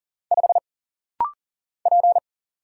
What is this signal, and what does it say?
Morse code, heard as a keyed tone, about 40 wpm, 700 Hz. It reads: 5 P
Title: Single Letter-Number - Rapid-Fire 40wpm